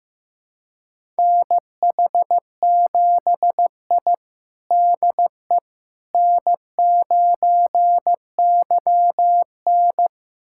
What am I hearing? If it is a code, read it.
Morse code, heard as a keyed tone, about 15 wpm, 700 Hz. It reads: NH7I DE N9YN
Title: USA Callsign DE Callsign - 4 Characters - 15wpm